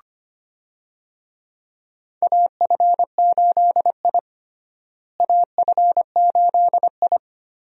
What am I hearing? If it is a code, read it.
Morse code, heard as a keyed tone, about 25 wpm, 700 Hz. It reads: AF8I AF8I
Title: USA Call Signs - 4 Characters - Repeated 3x - 25wpm